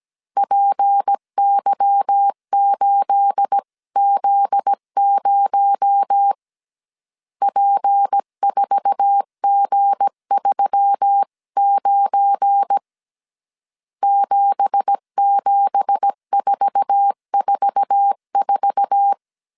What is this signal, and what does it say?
Morse code, heard as a keyed tone, about 17 wpm, 800 Hz. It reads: PY8Z0 P4G39 77444
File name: CW / DET / 1 9 1500C S C 16wpm 2909